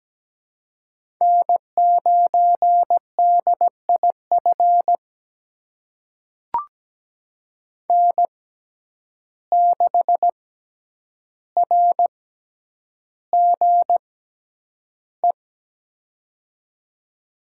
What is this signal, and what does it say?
Morse code, heard as a keyed tone, about 17 wpm, spaced as for 9 wpm, 700 Hz. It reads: N9DIF N6RGE
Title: USA Call Signs - 6x Character Spacing - 17wpm